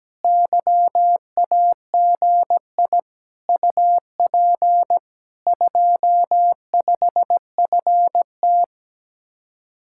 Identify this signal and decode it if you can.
Morse code, heard as a keyed tone, about 17 wpm, 700 Hz. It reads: YAGI UP 25FT
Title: Speed-Racing Common QSO Phrases Encore - 17wpm